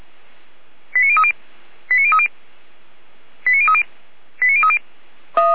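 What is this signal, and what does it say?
Five-tone selective call (ZVEI), heard as quick runs of five tones, tones of about 70 ms. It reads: 89030 89030 89030 89030
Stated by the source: ZVEI five-tone selective call 89080 received by SDR.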